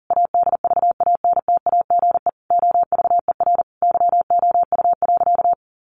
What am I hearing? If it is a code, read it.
Morse code, heard as a keyed tone, about 40 wpm, 700 Hz. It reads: ADVANTAGE OVER YOU.
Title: Sentences from Top 900 Words 40wpm